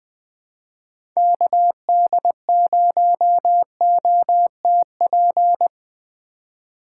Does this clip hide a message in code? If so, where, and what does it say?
Yes: Morse code from 1.2 to 5.7 s, heard as a keyed tone, about 19 wpm, 700 Hz. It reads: KD0OTP